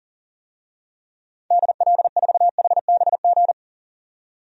Morse code, heard as a keyed tone, about 40 wpm, 700 Hz. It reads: DL4HBG
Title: DXCC-100 Call Signs - 6 Characters - 40wpm